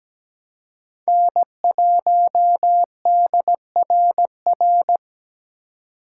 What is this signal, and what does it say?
Morse code, heard as a keyed tone, about 17 wpm, 700 Hz. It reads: N1DRR
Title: USA Call Signs - 5 Characters - 4x Character Spacing - 17wpm